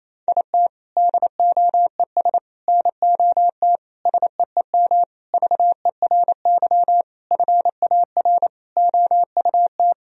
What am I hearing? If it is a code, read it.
Morse code, heard as a keyed tone, about 28 wpm, 700 Hz. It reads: IT DOES NOT SEEM VERY FAR OUT
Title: Sentences from Top 1000 Words - Review Part 1 - 28wpm